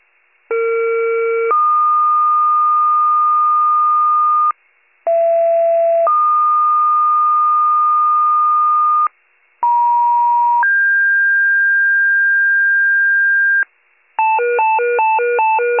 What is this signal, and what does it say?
Two-tone page, heard as a one-second tone, then a three-second tone, about 460 then 1220 Hz, about 670 then 1220 Hz, about 950 then 1600 Hz. It is heard from 0.5 to 13.6 s.